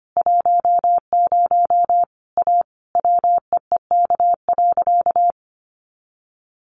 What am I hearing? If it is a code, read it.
Morse code, heard as a keyed tone, about 25 wpm, 700 Hz. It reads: 10 A WEEK.
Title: Easy Phrases 25wpm